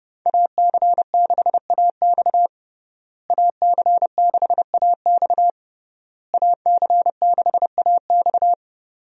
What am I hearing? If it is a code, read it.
Morse code, heard as a keyed tone, about 32 wpm, 700 Hz. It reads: AC6AX AC6AX AC6AX
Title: USA Call Signs - 5 Characters - Repeated 3x - 30wpm